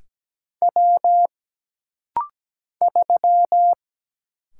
Morse code, heard as a keyed tone, about 17 wpm, 700 Hz. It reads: W 3
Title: Single Letter-Number Alt - Rapid-Fire 17wpm